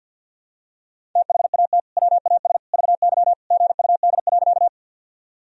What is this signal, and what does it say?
Morse code, heard as a keyed tone, about 50 wpm, 700 Hz. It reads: THAT WAS VY GUD.
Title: Sentences from Top 200 Words and Abbreviations 50wpm